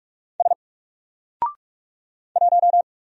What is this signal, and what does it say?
Morse code, heard as a keyed tone, about 48 wpm, 700 Hz. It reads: S 1
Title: Single Character (Letter:Number:Punctuation:Prosign) Alt 45wpm